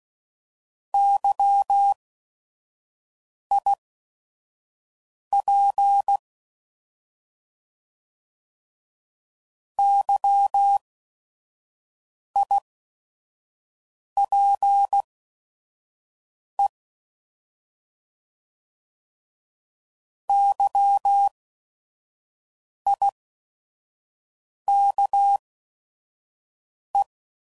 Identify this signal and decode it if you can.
Morse code, heard as a keyed tone, about 16 wpm, spaced as for 5 wpm, 800 Hz. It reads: YIP YIPE YIKE
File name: Prefixes / By First Letter / Ys